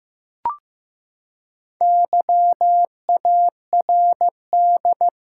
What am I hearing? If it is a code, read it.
Morse code, heard as a keyed tone, about 15 wpm, 700 Hz. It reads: YARD
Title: Four Letter Words - 15wpm